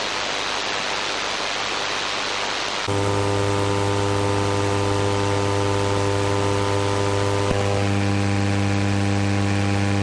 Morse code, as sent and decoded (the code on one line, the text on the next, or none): none